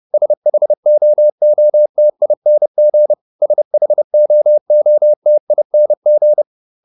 Morse code, heard as a keyed tone, about 30 wpm, 600 Hz. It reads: SHOOTING SHOOTING